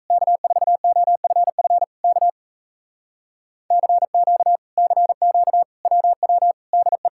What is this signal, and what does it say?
Morse code, heard as a keyed tone, about 38 wpm, 700 Hz. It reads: K4OFF K CQ CQ WW DE